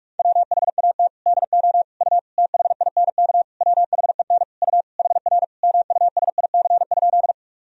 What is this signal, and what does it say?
Morse code, heard as a keyed tone, about 46 wpm, 700 Hz. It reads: WHAT DO U THINK WHEN U HR MUSIC?
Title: Sentences from Top 300 Words and Abbreviations 45wpm